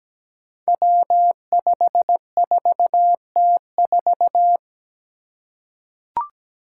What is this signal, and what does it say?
Morse code, heard as a keyed tone, about 17 wpm, 700 Hz. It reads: W54T4
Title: Five Challenging Letter-Number Combinations 17wpm